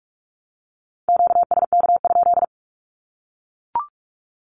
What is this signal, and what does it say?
Morse code, heard as a keyed tone, about 44 wpm, 700 Hz. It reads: QSK?
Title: Q-Codes Repeated 3x 45wpm